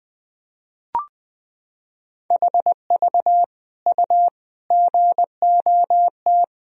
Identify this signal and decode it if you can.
Morse code, heard as a keyed tone, about 20 wpm, 700 Hz. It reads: HV U GOT
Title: Sentences from Top 300 Words and Abbreviations 20wpm